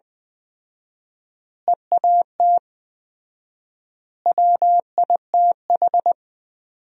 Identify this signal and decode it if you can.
Morse code, heard as a keyed tone, about 20 wpm, 700 Hz. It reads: EAT WITH
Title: 4x Word-Spacing - Sentences from Top 300 Words - 20wpm